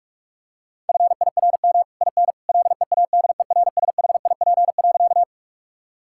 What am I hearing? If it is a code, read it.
Morse code, heard as a keyed tone, about 43 wpm, 700 Hz. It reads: FIRM IN LEADERSHIP.
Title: Sentences from Top 1600 Words 45wpm